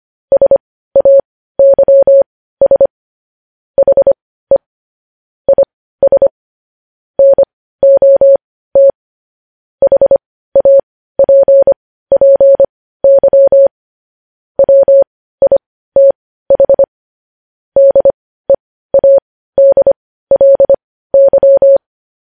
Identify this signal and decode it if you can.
Morse code, heard as a keyed tone, about 25 wpm, spaced as for 15 wpm, 550 Hz. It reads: SAYS HE IS NOT HAPPY WITH DEADLY
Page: News Headlines in Morse Code at 15 WPM